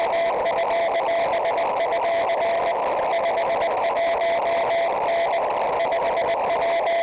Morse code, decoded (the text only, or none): none